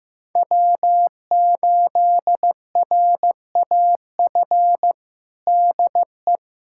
Morse code, heard as a keyed tone, about 15 wpm, 700 Hz. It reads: W8RAF DE
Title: USA Callsign DE Callsign - 5 Characters - 15wpm